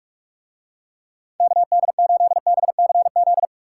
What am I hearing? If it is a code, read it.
Morse code, heard as a keyed tone, about 45 wpm, 700 Hz. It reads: KD8BCZ